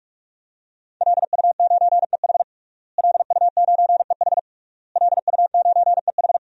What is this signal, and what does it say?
Morse code, heard as a keyed tone, about 41 wpm, 700 Hz. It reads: LU9EH LU9EH LU9EH
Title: DXCC-100 Call Signs - 5 Characters - Repeated 3x - 45wpm